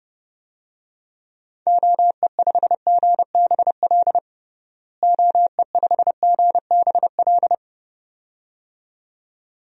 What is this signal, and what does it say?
Morse code, heard as a keyed tone, about 30 wpm, 700 Hz. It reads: OE5GBL OE5GBL
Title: DXCC-100 Call Signs - 6 Characters - Repeated 2x - 30wpm